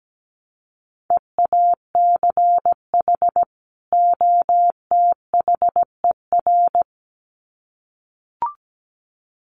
Morse code, heard as a keyed tone, about 17 wpm, 700 Hz. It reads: EACH OTHER
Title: Sets of 2 Words 17wpm